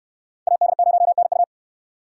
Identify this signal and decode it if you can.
Morse code, heard as a keyed tone, about 65 wpm, 700 Hz. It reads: AD0NU